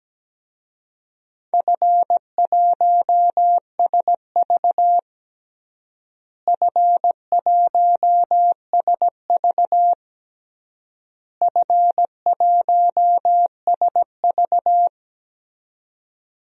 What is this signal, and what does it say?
Morse code, heard as a keyed tone, about 17 wpm, 700 Hz. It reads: F1SV F1SV F1SV
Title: DXCC-100 Call Signs - 4 Characters - Repeated 3x - 17wpm